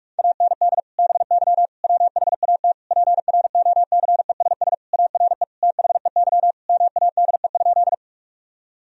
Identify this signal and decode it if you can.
Morse code, heard as a keyed tone, about 41 wpm, 700 Hz. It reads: AND BY WHAT PROCESS ARE THEY MADE?
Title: Sentences from Top 1000 Words - Review Part 3 - 45wpm